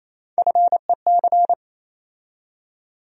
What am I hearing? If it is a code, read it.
Morse code, heard as a keyed tone, about 28 wpm, 700 Hz. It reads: FEC
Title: USA Call Sign Postfixes - 3 Letters 28wpm